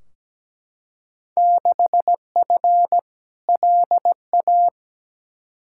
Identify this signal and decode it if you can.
Morse code, heard as a keyed tone, about 17 wpm, 700 Hz. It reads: 6F LA
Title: Field Day Exchanges Encore - 17wpm